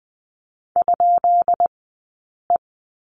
Morse code, heard as a keyed tone, about 20 wpm, 700 Hz. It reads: ? E
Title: 132 Lesson 22d - Question Mark - Two Characters 20wpm